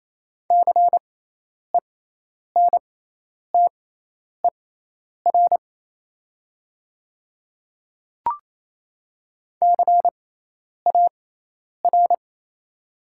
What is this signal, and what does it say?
Morse code, heard as a keyed tone, about 28 wpm, spaced as for 10 wpm, 700 Hz. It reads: CENTER CAR
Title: 6x Character Spacing - Top 500 Words - 28wpm